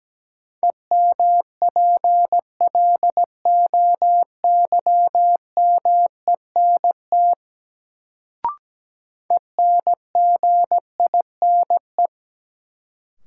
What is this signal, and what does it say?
Morse code, heard as a keyed tone, about 17 wpm, 700 Hz. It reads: EMPLOYMENT ENGINE
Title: Sentences from Top 1500 Words - New Word Drill Alt - 17wpm